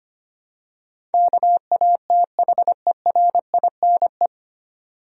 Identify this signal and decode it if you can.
Morse code, heard as a keyed tone, about 25 wpm, 700 Hz. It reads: KATHERINE
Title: USA Common Names - 25wpm